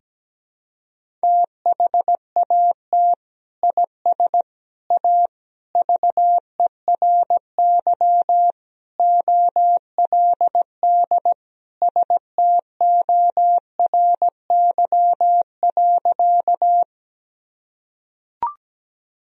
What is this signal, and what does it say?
Morse code, heard as a keyed tone, about 17 wpm, 700 Hz. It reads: THAT IS A VERY OLD STORY.
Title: Sentences from Top 1000 Words - Review Part 1 - 17wpm